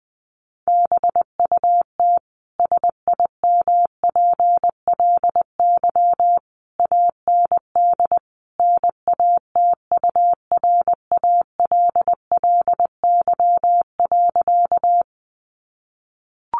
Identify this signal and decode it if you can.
Morse code, heard as a keyed tone, about 20 wpm, 700 Hz. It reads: BUT SIMPLY AND NATURALLY.
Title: Sentences from Top 1300 Words 20wpm